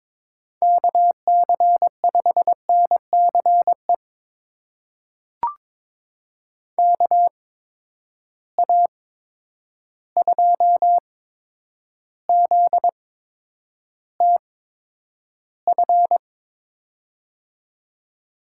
Morse code, heard as a keyed tone, about 22 wpm, spaced as for 9 wpm, 700 Hz. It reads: KC5NCE KA2ZTF
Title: USA Call Signs - 6 Characters - 8x Character Spacing - 22wpm